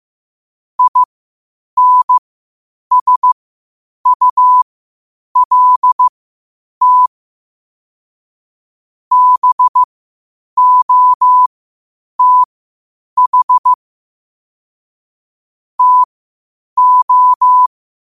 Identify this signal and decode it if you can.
Morse code, heard as a keyed tone, about 15 wpm, spaced as for 8 wpm, 1000 Hz. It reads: INSULT BOTH TO
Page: Quotes for Sat, 16 Aug 2025 in Morse Code at 8 words per minute.